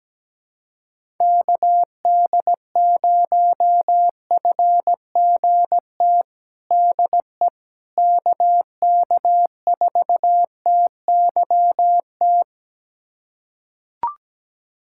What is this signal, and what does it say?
Morse code, heard as a keyed tone, about 17 wpm, 700 Hz. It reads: KD0FGT DE KK4TYT
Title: USA Callsign DE Callsign - 6 Characters - 17wpm